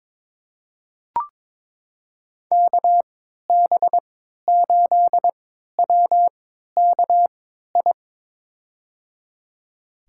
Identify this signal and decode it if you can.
Morse code, heard as a keyed tone, about 22 wpm, spaced as for 12 wpm, 700 Hz. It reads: KB8WKI
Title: USA Call Signs - 3x Character Spacing - 22wpm